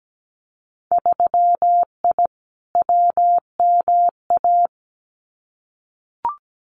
Morse code, heard as a keyed tone, about 17 wpm, 700 Hz. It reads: 3I WMA
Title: Winter Field Day Exchanges Encore 17wpm